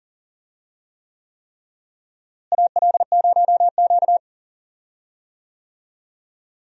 Morse code, heard as a keyed tone, about 40 wpm, 700 Hz. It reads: AL0Q
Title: USA Call Signs - 4 Characters - Repeated 3x - 40wpm